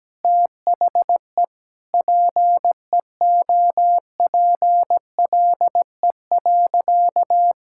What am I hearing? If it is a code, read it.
Morse code, heard as a keyed tone, about 17 wpm, 700 Hz. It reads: THE PEOPLE.
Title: Sentences from Top 1600 Words 17wpm